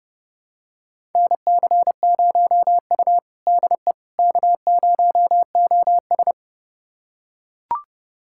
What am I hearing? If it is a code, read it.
Morse code, heard as a keyed tone, about 30 wpm, 700 Hz. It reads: NC0U DE K0OS
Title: USA Callsign DE Callsign - 4 Characters - 30wpm